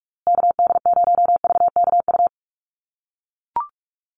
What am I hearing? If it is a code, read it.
Morse code, heard as a keyed tone, about 44 wpm, 700 Hz. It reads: KD0VKU
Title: USA Call Signs Repeated 2x 45wpm